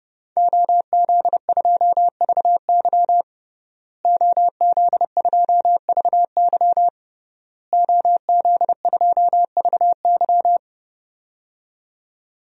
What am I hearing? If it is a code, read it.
Morse code, heard as a keyed tone, about 30 wpm, 700 Hz. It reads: OZ2VY OZ2VY OZ2VY
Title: DXCC-100 Call Signs - 5 Characters - Repeated 3x - 30wpm